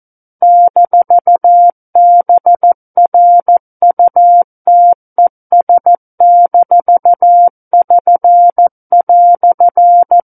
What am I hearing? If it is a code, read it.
Morse code, heard as a keyed tone, about 14 wpm, 700 Hz. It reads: -BRUTES-<SN>"
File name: Book One / 05-Chapter 5 - 14 WPM